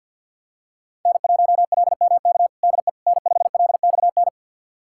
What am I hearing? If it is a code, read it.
Morse code, heard as a keyed tone, about 48 wpm, 700 Hz. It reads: N1LMK DE N5LXN
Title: USA Callsign DE Callsign - 5 Characters - 50wpm